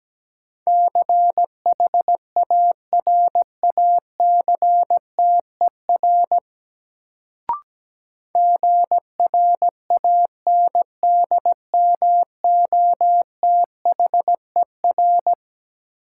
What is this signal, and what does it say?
Morse code, heard as a keyed tone, about 17 wpm, 700 Hz. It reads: CHARACTER GRANDMOTHER
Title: Sentences from Top 1200 Words - New Word Drill Alt - 17wpm